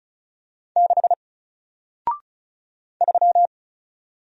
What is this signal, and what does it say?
Morse code, heard as a keyed tone, about 36 wpm, 700 Hz. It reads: 6 3
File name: Single-Letter-Number-35wpm